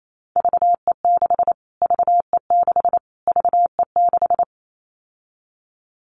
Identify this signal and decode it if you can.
Morse code, heard as a keyed tone, about 28 wpm, 700 Hz. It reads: VE6 VE6 VE6
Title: Callsign Prefixes for Top 25 DXCC - Repeated 3x - 28wpm